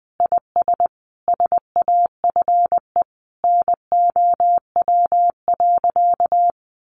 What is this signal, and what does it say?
Morse code, heard as a keyed tone, about 20 wpm, 700 Hz. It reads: IS SAFE NOW.
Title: Sentences from Top 900 Words 20wpm